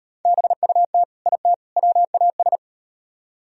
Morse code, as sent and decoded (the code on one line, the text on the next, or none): -... ..- - / .. - / .-- .- ...
BUT IT WAS